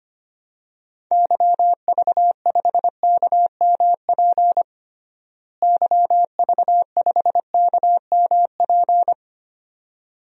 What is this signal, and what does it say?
Morse code, heard as a keyed tone, about 25 wpm, 700 Hz. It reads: YV5KMP YV5KMP
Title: DXCC-100 Call Signs - 6 Characters - Repeated 2x - 25wpm